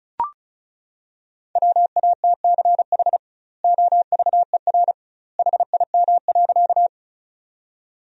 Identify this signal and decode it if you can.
Morse code, heard as a keyed tone, about 36 wpm, 700 Hz. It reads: WATCH OVER HIM.
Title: 2x Word-Spacing - Sentences from Top 300 Words - 35wpm